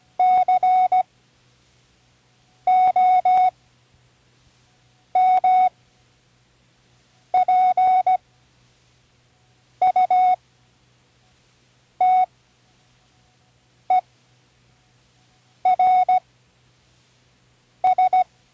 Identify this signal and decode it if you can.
Morse code, heard as a keyed tone, about 16 wpm, spaced as for 5 wpm, 740 Hz. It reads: COMPUTERS
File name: Learning Morse Code (from Tune in the World)